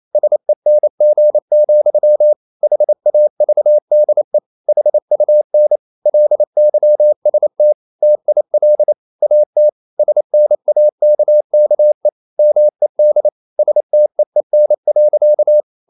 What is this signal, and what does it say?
Morse code, heard as a keyed tone, about 28 wpm, 600 Hz. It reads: SENG, HAVDE HUN LYST TIL AT SNAKKE MED STEEN.
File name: Ferie QRQ 28wpm